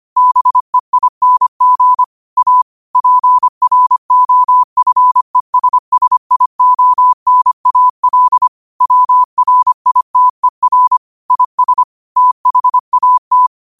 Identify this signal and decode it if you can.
Morse code, heard as a keyed tone, about 25 wpm, 1000 Hz. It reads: BEING A PROFESSIONAL WRITER IS THAT
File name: qotdfiles / 25wpm